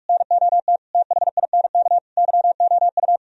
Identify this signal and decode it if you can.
Morse code, heard as a keyed tone, about 49 wpm, 700 Hz. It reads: NOT THINK YOU